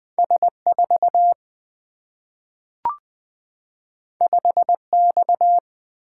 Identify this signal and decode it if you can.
Morse code, heard as a keyed tone, about 20 wpm, 700 Hz. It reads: S 4 5 X
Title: Single-Letter Single-Number 20wpm